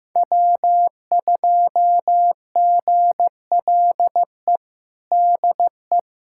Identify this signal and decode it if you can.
Morse code, heard as a keyed tone, about 15 wpm, 700 Hz. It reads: W2GLE DE
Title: USA Callsign DE Callsign 15wpm